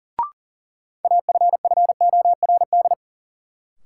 Morse code, heard as a keyed tone, about 40 wpm, 700 Hz. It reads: AFFORD
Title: Sentences from Top 1100 Words - New Word Drill Alt - 40wpm